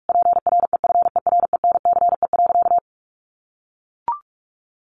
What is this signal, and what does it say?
Morse code, heard as a keyed tone, about 44 wpm, 700 Hz. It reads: PREFERENCE.
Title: Sentences from Top 1500 Words 45wpm